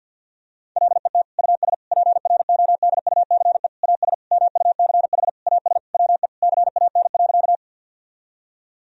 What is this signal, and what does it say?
Morse code, heard as a keyed tone, about 50 wpm, 700 Hz. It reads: LET US PRODUCE AS MUCH AS WE CAN.
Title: Sentences from Top 1000 Words - Review Part 3 - 50wpm